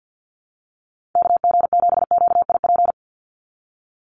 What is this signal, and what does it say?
Morse code, heard as a keyed tone, about 50 wpm, 700 Hz. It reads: KG7QIP
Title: USA Call Signs 50wpm